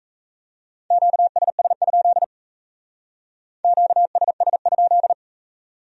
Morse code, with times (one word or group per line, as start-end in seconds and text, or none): 0.9-2.2 QSS?
3.6-5.1 QSS?